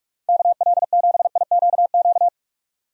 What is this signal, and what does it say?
Morse code, heard as a keyed tone, about 42 wpm, 700 Hz. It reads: KL7IQQ
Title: DXCC-100 Call Signs - 6 Characters - 45wpm